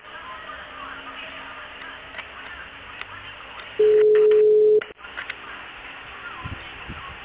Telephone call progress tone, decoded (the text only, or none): ringback tone